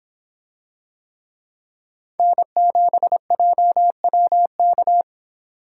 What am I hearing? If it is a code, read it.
Morse code, heard as a keyed tone, about 26 wpm, 700 Hz. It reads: N7JWK